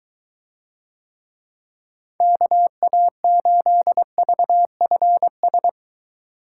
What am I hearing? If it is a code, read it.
Morse code, heard as a keyed tone, about 20 wpm, 700 Hz. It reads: KA8VFS